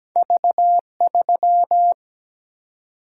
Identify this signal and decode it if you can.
Morse code, heard as a keyed tone, about 17 wpm, 700 Hz. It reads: V3
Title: Single-Letter Single-Number 17wpm